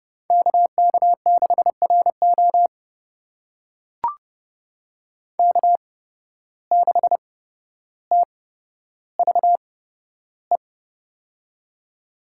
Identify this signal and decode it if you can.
Morse code, heard as a keyed tone, about 30 wpm, spaced as for 13 wpm, 700 Hz. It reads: KK6RO K6TVE
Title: USA Call Signs - 5 Characters - 8x Character Spacing - 30wpm